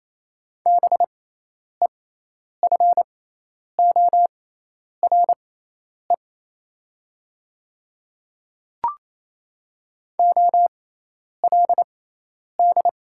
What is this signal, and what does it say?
Morse code, heard as a keyed tone, about 28 wpm, spaced as for 10 wpm, 700 Hz. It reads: BEFORE OLD
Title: 6x Character Spacing - Top 200 Words - 28wpm